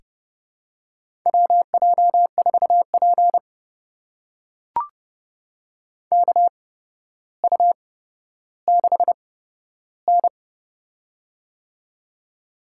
Morse code, heard as a keyed tone, about 31 wpm, spaced as for 13 wpm, 700 Hz. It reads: WJ4P KU6N
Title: USA Call Signs - 4 Characters - 8x Character Spacing - 30wpm